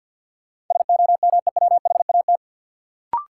Morse code, heard as a keyed tone, about 53 wpm, 700 Hz. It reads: SOMEWHAT
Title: Sentences from Top 1000 Words - Review Part 1 - New Word Drill - 50wpm